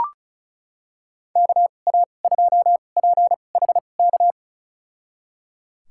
Morse code, heard as a keyed tone, about 36 wpm, spaced as for 25 wpm, 700 Hz. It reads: KA2PHK
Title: USA Call Signs - 6 Characters - 2x Character Spacing - 35wpm